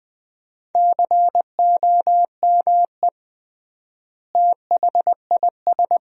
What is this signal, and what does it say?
Morse code, heard as a keyed tone, about 20 wpm, 700 Hz. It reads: COME THIS
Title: 3x Word-Spacing - Sentences from Top 100 Words - 20wpm